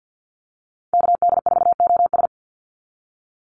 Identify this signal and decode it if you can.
Morse code, heard as a keyed tone, about 50 wpm, 700 Hz. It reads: KD4OS
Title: Call Signs Repeated 3x - 50wpm